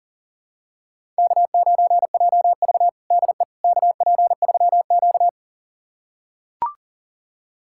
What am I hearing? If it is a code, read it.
Morse code, heard as a keyed tone, about 40 wpm, 700 Hz. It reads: K9JV DE KP3Q
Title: USA Callsign DE Callsign - 4 Characters - 40wpm